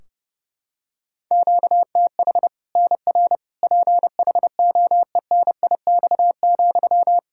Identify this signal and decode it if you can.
Morse code, heard as a keyed tone, about 29 wpm, 700 Hz. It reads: QTH NR PHOENIX,
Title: Practice QSO - Exchange 1 - Beginning - 30wpm